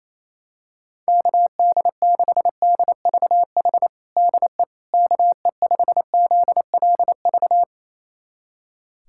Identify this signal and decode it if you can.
Morse code, heard as a keyed tone, about 28 wpm, 700 Hz. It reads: KD6DVH DE KE5ZLV
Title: USA Callsign DE Callsign 28wpm